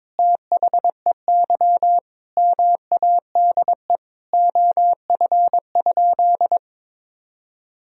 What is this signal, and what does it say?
Morse code, heard as a keyed tone, about 22 wpm, 700 Hz. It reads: THEY MADE OF?
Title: Sentences from Top 200 Words and Abbreviations 22wpm